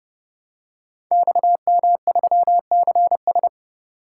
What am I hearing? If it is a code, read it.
Morse code, heard as a keyed tone, about 30 wpm, 700 Hz. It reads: XM3CS
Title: Code Groups - 5 Letter:Numbers - 30wpm